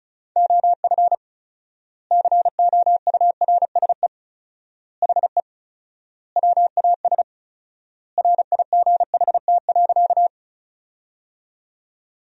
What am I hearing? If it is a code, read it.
Morse code, heard as a keyed tone, about 34 wpm, 700 Hz. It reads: OF COURSE HE WAS RIGHT.
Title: 4x Word-Spacing - Sentences from Top 500 Words - 35wpm